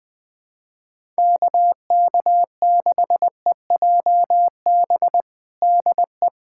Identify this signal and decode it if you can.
Morse code, heard as a keyed tone, about 20 wpm, 700 Hz. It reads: KK6EJB DE
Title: USA Callsign DE Callsign - 6 Characters - 20wpm